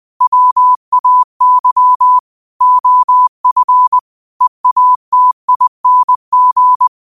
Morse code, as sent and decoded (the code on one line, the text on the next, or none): .-- .- -.-- / --- ..-. / . .- - .. -. --.
WAY OF EATING